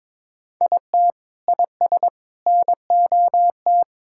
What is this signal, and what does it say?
Morse code, heard as a keyed tone, about 22 wpm, 700 Hz. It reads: IT IS NOT
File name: practice / 22wpm / Sentences with 2-Word Phrases - 22wpm